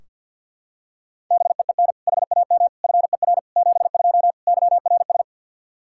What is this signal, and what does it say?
Morse code, heard as a keyed tone, about 46 wpm, 700 Hz. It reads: BEEN HAM FER 72 YRS
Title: K1USN SST QSO Practice 50wpm